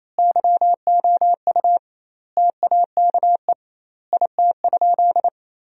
Morse code, heard as a keyed tone, about 28 wpm, 700 Hz. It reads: YOU TAKE IT?
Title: 2x Word-Spacing - Sentences from Top 200 Words - 28wpm